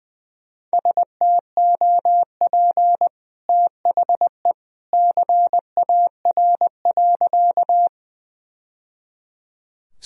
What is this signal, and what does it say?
Morse code, heard as a keyed tone, about 20 wpm, 700 Hz. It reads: STOP THE CAR.
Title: Sentences from Top 1000 Words - Review Part 1 - 20wpm